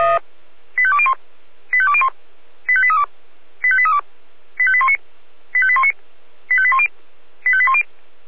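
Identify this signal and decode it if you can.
Five-tone selective call (ZVEI), heard as quick runs of five tones, tones of about 70 ms. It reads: 85201 85201 85832 85832 85819 85819 85810 85810
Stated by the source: ZVEI five-tone selective call received on SDR.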